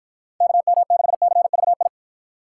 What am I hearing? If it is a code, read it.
Morse code, heard as a keyed tone, about 52 wpm, 700 Hz. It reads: KM6CVI